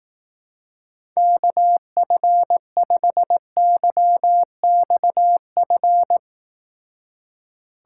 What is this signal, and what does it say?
Morse code, heard as a keyed tone, about 18 wpm, 700 Hz. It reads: KF5YXF
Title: Speed Racing - USA 2x3 Call Signs 15wpm